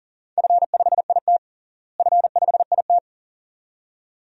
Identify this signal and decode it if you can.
Morse code, heard as a keyed tone, about 40 wpm, 700 Hz. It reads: F5IT F5IT
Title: DXCC-100 Call Signs - 4 Characters - Repeated 2x - 40wpm